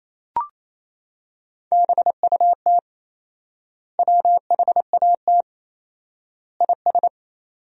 Morse code, heard as a keyed tone, about 28 wpm, 700 Hz. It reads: BUT WHAT IS
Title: 4x Word-Spacing - Sentences from Top 100 Words - 28wpm